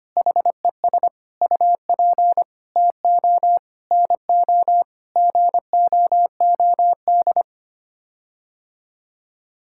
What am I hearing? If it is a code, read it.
Morse code, heard as a keyed tone, about 25 wpm, 700 Hz. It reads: HES UP TO NO GOOD
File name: Sets of 9 Words - 25wpm